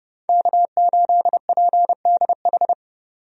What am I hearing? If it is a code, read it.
Morse code, heard as a keyed tone, about 31 wpm, 700 Hz. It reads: K8PDH
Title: USA Call Signs - 5 Characters - 30wpm